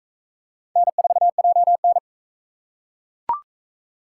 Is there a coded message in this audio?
Yes: Morse code from 0.8 to 2.0 s, heard as a keyed tone, about 40 wpm, 700 Hz. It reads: N4JN